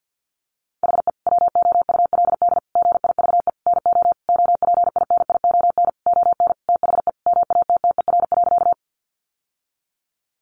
Morse code, heard as a keyed tone, about 50 wpm, 700 Hz. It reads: HE WOULD GIVE NO OPINION ON THE MATTER.
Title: Sentences from Top 1200 Words 50wpm